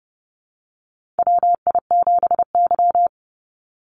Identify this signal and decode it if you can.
Morse code, heard as a keyed tone, about 31 wpm, 700 Hz. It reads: WI7Y
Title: USA Call Signs 30wpm